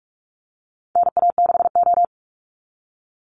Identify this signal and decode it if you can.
Morse code, heard as a keyed tone, about 44 wpm, 700 Hz. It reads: NA6O
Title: NA SOTA Chaser Callsigns 45wpm